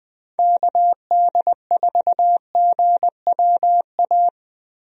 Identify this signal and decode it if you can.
Morse code, heard as a keyed tone, about 20 wpm, 700 Hz. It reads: KD4GWA
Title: USA Call Signs - 6 Characters - Repeated 2x - 20wpm